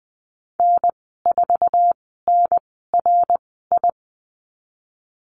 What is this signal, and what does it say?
Morse code, heard as a keyed tone, about 20 wpm, spaced as for 15 wpm, 700 Hz. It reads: N4NRI
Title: 062 Lesson 11g - Letter h - Call Signs 20wpm